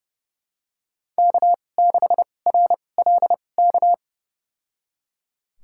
Morse code, heard as a keyed tone, about 31 wpm, spaced as for 22 wpm, 700 Hz. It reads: K6RLK
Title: USA Call Signs - 2x Character Spacing - 30wpm